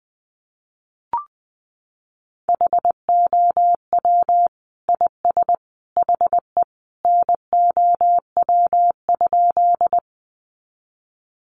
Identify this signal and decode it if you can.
Morse code, heard as a keyed tone, about 20 wpm, 700 Hz. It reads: HOW IS HE NOW?
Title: Sentences from Top 100 Words 20wpm